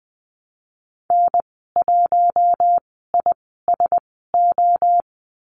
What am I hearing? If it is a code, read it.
Morse code, heard as a keyed tone, about 20 wpm, spaced as for 14 wpm, 700 Hz. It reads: N1ISO
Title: 062 Lesson 11g - Letter h - Call Signs 20wpm